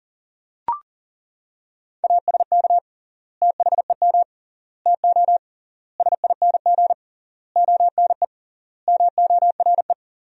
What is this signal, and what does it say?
Morse code, heard as a keyed tone, about 40 wpm, 700 Hz. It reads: ASK THEM TO SING ONE MORE
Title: 3x Word-Spacing - Sentences from Top 400 Words - 40wpm